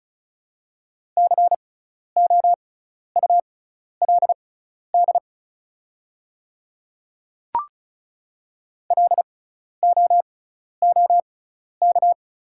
Morse code, heard as a keyed tone, about 36 wpm, spaced as for 12 wpm, 700 Hz. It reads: COULD LOOK
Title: 6x Character Spacing - Top 200 Words - 35wpm